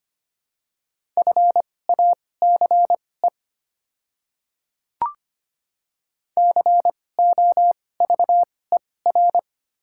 Morse code, heard as a keyed tone, about 25 wpm, spaced as for 18 wpm, 700 Hz. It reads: FACE COVER
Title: 2x Character Spacing - Top 300 Words - 25wpm